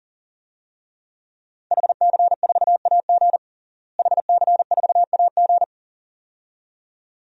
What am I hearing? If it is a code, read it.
Morse code, heard as a keyed tone, about 40 wpm, 700 Hz. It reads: HC4AG HC4AG
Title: DXCC-100 Call Signs - 5 Characters - Repeated 2x - 40wpm